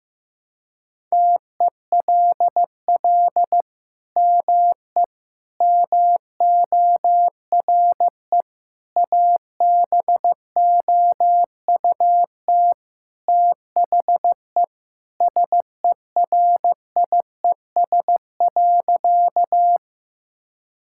Morse code, heard as a keyed tone, about 15 wpm, 700 Hz. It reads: TELL ME MORE ABOUT THE SERIES.
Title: Sentences from Top 1000 Words - Review Part 5 - 15wpm